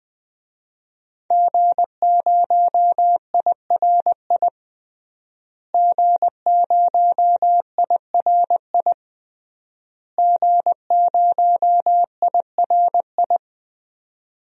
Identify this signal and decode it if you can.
Morse code, heard as a keyed tone, about 20 wpm, 700 Hz. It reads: G0IRI G0IRI G0IRI
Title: DXCC-100 Call Signs - 5 Characters - Repeated 3x - 20wpm